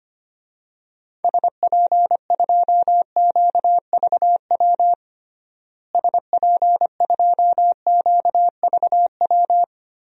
Morse code, heard as a keyed tone, about 25 wpm, 700 Hz. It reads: SP2QVW SP2QVW
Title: DXCC-100 Call Signs - 6 Characters - Repeated 3x - 25wpm